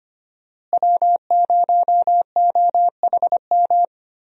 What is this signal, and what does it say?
Morse code, heard as a keyed tone, about 25 wpm, 700 Hz. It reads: W0OHM
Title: USA Call Signs - 5 Characters - Repeated 2x - 25wpm